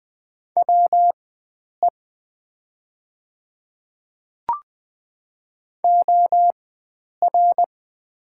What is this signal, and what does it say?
Morse code, heard as a keyed tone, about 20 wpm, spaced as for 9 wpm, 690 Hz. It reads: WE OR
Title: 4x Character Spacing - Top 100 Words - 20wpm